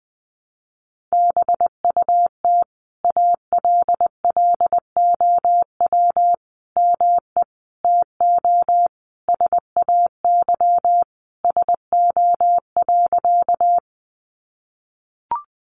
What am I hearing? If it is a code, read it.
Morse code, heard as a keyed tone, about 20 wpm, 700 Hz. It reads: BUT ALLOW ME TO SAY SO.
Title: Sentences from Top 700 Words 20wpm